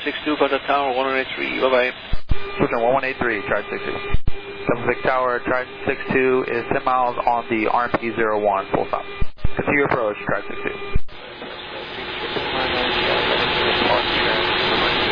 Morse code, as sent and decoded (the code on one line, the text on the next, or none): none